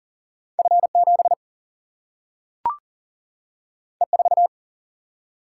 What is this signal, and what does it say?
Morse code, heard as a keyed tone, about 38 wpm, 700 Hz. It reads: F 7 E 4
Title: Single-Letter Single-Number 40wpm